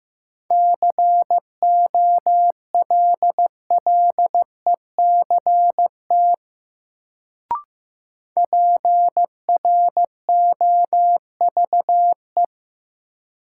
Morse code, heard as a keyed tone, about 15 wpm, 700 Hz. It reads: COLLECT PROVE
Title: Sentences from Top 1600 Words - New Word Drill Alt - 15wpm